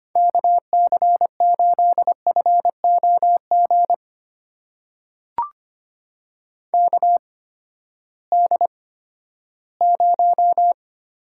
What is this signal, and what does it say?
Morse code, heard as a keyed tone, about 25 wpm, spaced as for 14 wpm, 700 Hz. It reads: KC8FOG KD0
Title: USA Call Signs - 6 Characters - 8x Character Spacing - 25wpm